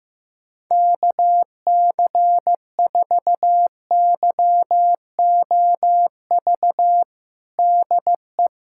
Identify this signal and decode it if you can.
Morse code, heard as a keyed tone, about 15 wpm, 700 Hz. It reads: KC4YOV DE
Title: USA Callsign DE Callsign - 6 Characters - 15wpm